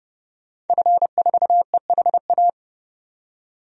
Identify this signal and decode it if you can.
Morse code, heard as a keyed tone, about 30 wpm, 700 Hz. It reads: F4EHA
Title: DXCC-100 Call Signs - 5 Characters - 30wpm